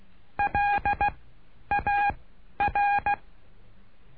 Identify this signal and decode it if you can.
Morse code, heard as a keyed tone, about 16 wpm, spaced as for 10 wpm, 820 Hz. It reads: LAR
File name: cursos / CURSO DE TELEGRAFIA / Telegrafia - Trilha 09